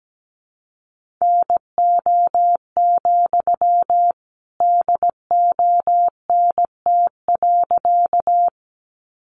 Morse code, heard as a keyed tone, about 17 wpm, 700 Hz. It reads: NO, DONT.
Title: Sentences from Top 300 Words 17wpm